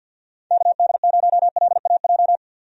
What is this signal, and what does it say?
Morse code, heard as a keyed tone, about 50 wpm, 700 Hz. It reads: KD0LAJ
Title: USA Call Signs - 6 Characters - 50wpm